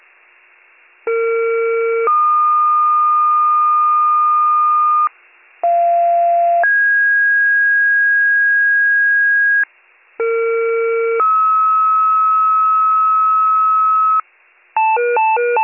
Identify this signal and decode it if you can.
Two-tone page, heard as a one-second tone, then a three-second tone, about 460 then 1220 Hz, about 690 then 1650 Hz, about 460 then 1290 Hz. It is heard from 1.1 to 14.2 s.